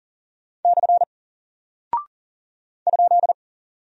Morse code, heard as a keyed tone, about 40 wpm, 700 Hz. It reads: / ?
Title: Single Character (Letter:Number:Punctuation:Prosign) Alt 40wpm